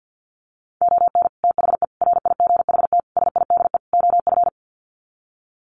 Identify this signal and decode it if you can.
Morse code, heard as a keyed tone, about 50 wpm, 700 Hz. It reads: ON THE RIGHT SIDE OF
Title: Sets of 5 Words Encore 50wpm